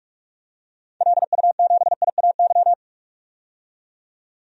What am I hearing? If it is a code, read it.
Morse code, heard as a keyed tone, about 43 wpm, 700 Hz. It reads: LU7IAY
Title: DXCC-100 Call Signs - 6 Characters - 45wpm